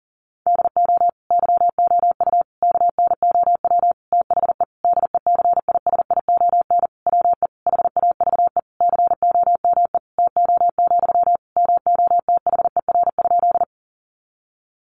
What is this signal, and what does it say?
Morse code, heard as a keyed tone, about 40 wpm, 700 Hz. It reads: DO YOU KNOW THE DECISION WE HAVE COME TO, MOTHER?
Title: Sentences from Top 1000 Words 40wpm